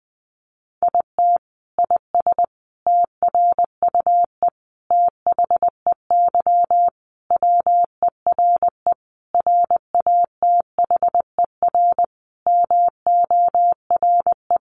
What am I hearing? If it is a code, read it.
Morse code, heard as a keyed tone, about 20 wpm, 700 Hz. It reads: IT IS TRUE THEY WERE RATHER MORE
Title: Sentences from Top 1400 Words 20wpm